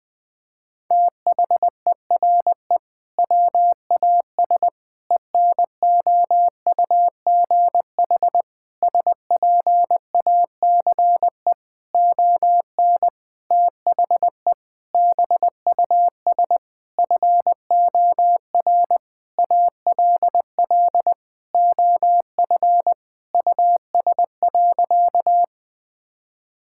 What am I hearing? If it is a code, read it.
Morse code, heard as a keyed tone, about 20 wpm, 700 Hz. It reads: THERE WAS ENOUGH SPACE ON THE BUS FOR ALL OF US.
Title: Sentences from Top 1000 Words - Review Part 6 - 20wpm